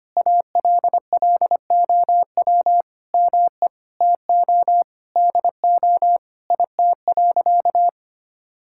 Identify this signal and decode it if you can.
Morse code, heard as a keyed tone, about 25 wpm, 700 Hz. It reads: ALLOW ME TO DO IT.
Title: Sentences from Top 1000 Words - Review Part 5 - 25wpm